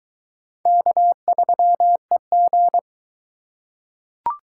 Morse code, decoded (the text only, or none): K3EG